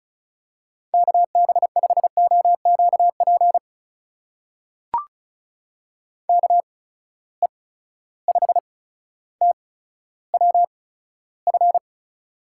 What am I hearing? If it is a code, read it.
Morse code, heard as a keyed tone, about 36 wpm, spaced as for 15 wpm, 700 Hz. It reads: KB5OQP KE5TWF
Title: USA Call Signs - 6 Characters - 8x Character Spacing - 35wpm